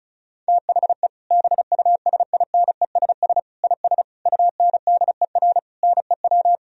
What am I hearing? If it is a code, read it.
Morse code, heard as a keyed tone, about 34 wpm, 700 Hz. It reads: THE BUSINESS IS UNDER NEW
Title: Sentences from Top 1000 Words - Review Part 4 - 35wpm